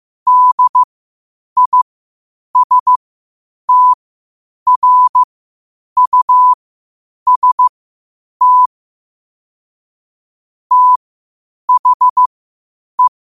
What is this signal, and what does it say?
Morse code, heard as a keyed tone, about 15 wpm, spaced as for 8 wpm, 1000 Hz. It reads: DISTRUST THE